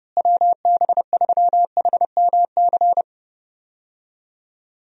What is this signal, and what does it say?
Morse code, heard as a keyed tone, about 31 wpm, 700 Hz. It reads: WB3HMC
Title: USA Call Signs - 6 Characters - Repeated 2x - 30wpm